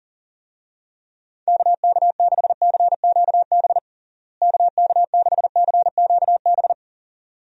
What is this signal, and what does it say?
Morse code, heard as a keyed tone, about 37 wpm, 700 Hz. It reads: KK6CQB KK6CQB
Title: USA Call Signs - 6 Characters - Repeated 2x - 40wpm